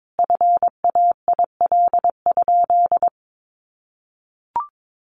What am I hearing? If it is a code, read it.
Morse code, heard as a keyed tone, about 22 wpm, 700 Hz. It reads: FAIL?
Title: Sentences from Top 1100 Words 22wpm